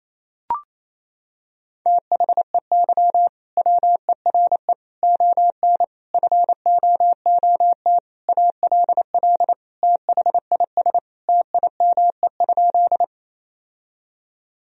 Morse code, heard as a keyed tone, about 27 wpm, 690 Hz. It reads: THEY WERE ON FOOT ALL THIS TIME?
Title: Sentences from Top 1000 Words - Review Part 3 - 28wpm